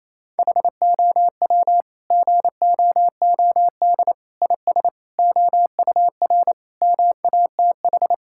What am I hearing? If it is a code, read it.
Morse code, heard as a keyed tone, about 28 wpm, 700 Hz. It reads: HOW GOOD IS OUR MATH
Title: Sentences from Top 1000 Words - Review Part 6 - 28wpm